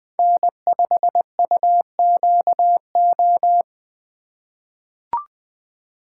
Morse code, heard as a keyed tone, about 20 wpm, 700 Hz. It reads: N5UQO
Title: USA Call Signs - 5 Characters - 3x Character Spacing - 20wpm